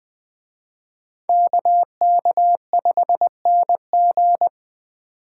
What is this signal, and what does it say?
Morse code, heard as a keyed tone, about 21 wpm, 700 Hz. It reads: KK5NG
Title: Speed Racing - USA 2x2 Call Signs 17wpm